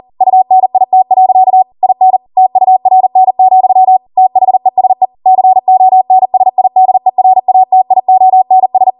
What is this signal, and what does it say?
Morse code, heard as a keyed tone, about 40 wpm, 750 Hz. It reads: UNIT. IN TURN, THESE CONSIDERATIONS